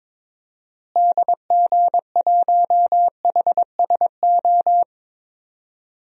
Morse code, heard as a keyed tone, about 22 wpm, 700 Hz. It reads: DG1HSO
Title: DXCC-100 Call Signs - 6 Characters - Repeated 2x - 22wpm